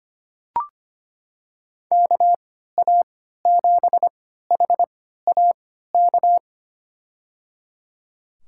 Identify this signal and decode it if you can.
Morse code, heard as a keyed tone, about 25 wpm, spaced as for 14 wpm, 700 Hz. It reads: KA7HAK
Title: USA Call Signs - 3x Character Spacing - 25wpm